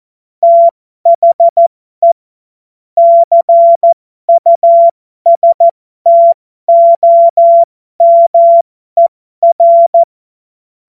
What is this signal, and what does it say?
Morse code, heard as a keyed tone, about 14 wpm, spaced as for 12 wpm, 680 Hz. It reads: THE CUSTOMER
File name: Morsekurs / Witze